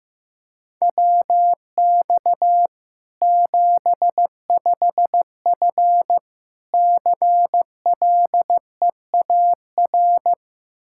Morse code, heard as a keyed tone, about 15 wpm, 700 Hz. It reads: WX 75F CLEAR
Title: Practice QSO - Exchange 2 - Info - 15wpm